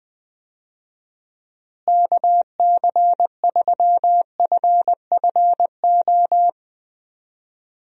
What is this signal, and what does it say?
Morse code, heard as a keyed tone, about 20 wpm, 700 Hz. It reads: KC3FFO
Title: USA Call Signs - 6 Characters - Repeated 3x - 20wpm